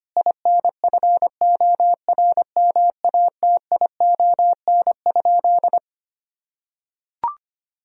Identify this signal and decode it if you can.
Morse code, heard as a keyed tone, about 25 wpm, 700 Hz. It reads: INFORMATION?
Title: Sentences from Top 1000 Words - Review Part 4 - 25wpm